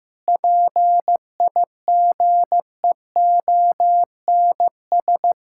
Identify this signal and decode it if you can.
Morse code, heard as a keyed tone, about 15 wpm, 700 Hz. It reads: PIGEONS
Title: Pangram Challenge v2 - 15wpm